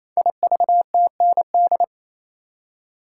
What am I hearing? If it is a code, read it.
Morse code, heard as a keyed tone, about 29 wpm, 700 Hz. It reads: IVTND
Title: Code Groups - 5 Letters - 28wpm